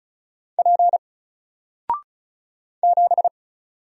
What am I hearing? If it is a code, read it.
Morse code, heard as a keyed tone, about 36 wpm, 700 Hz. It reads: P 7
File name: Single-Letter-Number-35wpm